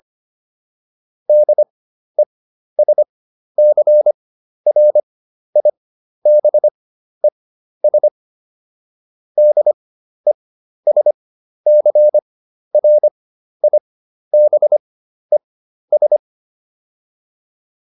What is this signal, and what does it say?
Morse code, heard as a keyed tone, about 25 wpm, spaced as for 12 wpm, 600 Hz. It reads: DESCRIBES DESCRIBES